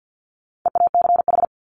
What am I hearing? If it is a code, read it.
Morse code, heard as a keyed tone, about 48 wpm, 700 Hz. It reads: EACH
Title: Top 200 Words 50wpm